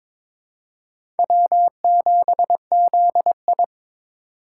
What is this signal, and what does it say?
Morse code, heard as a keyed tone, about 22 wpm, 700 Hz. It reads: W7ZI
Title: USA Call Signs - 4 Characters - 4x Character Spacing - 22wpm